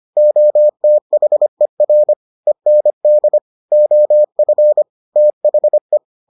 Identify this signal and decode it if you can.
Morse code, heard as a keyed tone, about 25 wpm, 600 Hz. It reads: OTHER END OF THE